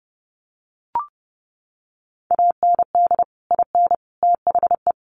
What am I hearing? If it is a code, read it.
Morse code, heard as a keyed tone, about 30 wpm, 700 Hz. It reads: AND IN THE
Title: Sets of 3 Words 30wpm